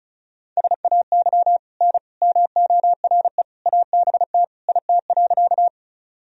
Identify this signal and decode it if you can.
Morse code, heard as a keyed tone, about 36 wpm, 690 Hz. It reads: SAY N MORE ABT IT.
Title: Sentences from Top 200 Words and Abbreviations 35wpm